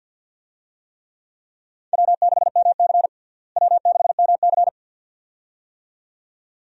Morse code, heard as a keyed tone, about 50 wpm, 700 Hz. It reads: W6MC W6MC
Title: USA Call Signs - 4 Characters - Repeated 2x - 50wpm